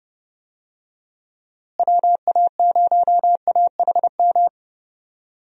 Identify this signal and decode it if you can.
Morse code, heard as a keyed tone, about 30 wpm, 700 Hz. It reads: WA0AHM
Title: USA Call Signs - 6 Characters - Repeated 3x - 30wpm